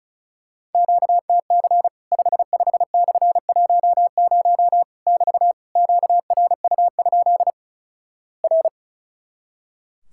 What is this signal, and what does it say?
Morse code, heard as a keyed tone, about 36 wpm, 700 Hz. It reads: QTC 55/10 <BT> QRU? R
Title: Worked All Europe DX Contest - QTC Exchange - 35wpm